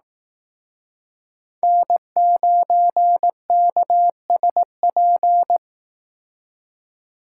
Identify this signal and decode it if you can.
Morse code, heard as a keyed tone, about 18 wpm, 700 Hz. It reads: N9KSP
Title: Speed Racing - USA 1x3 Call Signs 15wpm